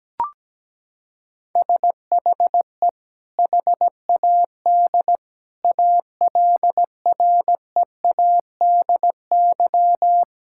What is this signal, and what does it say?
Morse code, heard as a keyed tone, about 17 wpm, 700 Hz. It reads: SHE HAD ALREADY
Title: Sentences from Top 1000 Words - Review Part 6 - 17wpm